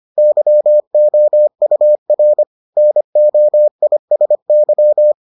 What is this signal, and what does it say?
Morse code, heard as a keyed tone, about 25 wpm, 600 Hz. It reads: YOUR NOISY